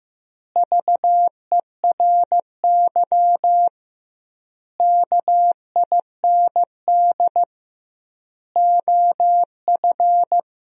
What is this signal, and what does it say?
Morse code, heard as a keyed tone, about 15 wpm, 700 Hz. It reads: VERY KIND OF
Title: 2x Word-Spacing - Sentences from Top 200 Words - 15wpm